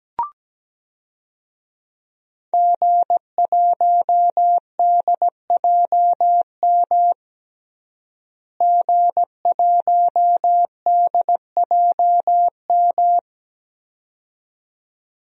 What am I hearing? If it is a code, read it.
Morse code, heard as a keyed tone, about 17 wpm, 700 Hz. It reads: G1DJM G1DJM
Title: DXCC-100 Call Signs - 5 Characters - Repeated 2x - 17wpm